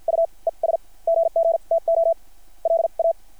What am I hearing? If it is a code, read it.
Morse code, heard as a keyed tone, about 53 wpm, spaced as for 38 wpm, 690 Hz. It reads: UES GGTO LA
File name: archived / CRSnet